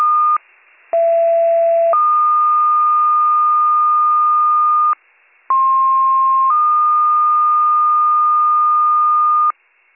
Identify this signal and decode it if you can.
Two-tone page, heard as a one-second tone, then a three-second tone, about 670 then 1220 Hz, about 1060 then 1220 Hz. It is heard from 0.9 to 9.5 s.